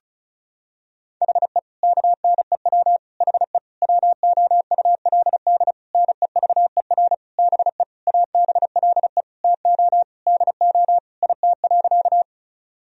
Morse code, heard as a keyed tone, about 34 wpm, 700 Hz. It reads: HE KNEW HE WOULD NEVER BE ABLE TO DO IT.
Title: Sentences from Top 1000 Words - Review Part 2 - 35wpm